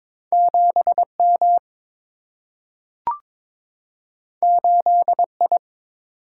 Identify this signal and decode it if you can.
Morse code, heard as a keyed tone, about 22 wpm, 700 Hz. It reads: 7M 8I
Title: Single-Letter Single-Number 22wpm